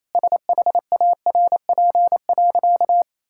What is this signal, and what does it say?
Morse code, heard as a keyed tone, about 28 wpm, 700 Hz. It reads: SHARP.
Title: Sentences from Top 1600 Words 28wpm